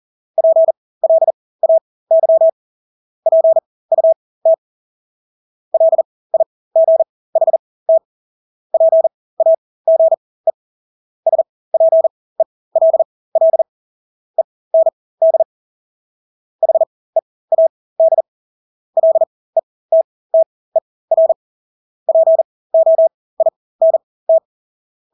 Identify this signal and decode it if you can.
Morse code, heard as a keyed tone, about 40 wpm, spaced as for 20 wpm, 650 Hz. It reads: PLAY PUT LIGHT PAGE SPELL END HEAD LETTER POINT
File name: Engelske ord 40-20 wpm